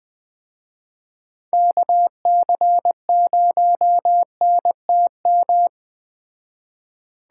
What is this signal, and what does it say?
Morse code, heard as a keyed tone, about 20 wpm, 700 Hz. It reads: KC0NTM
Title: Speed Racing - USA 2x3 Call Signs 17wpm